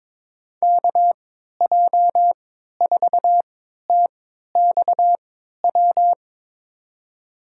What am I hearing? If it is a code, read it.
Morse code, heard as a keyed tone, about 21 wpm, spaced as for 12 wpm, 700 Hz. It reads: KJ4TXW
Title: USA Call Signs - 6 Characters - 3x Character Spacing - 22wpm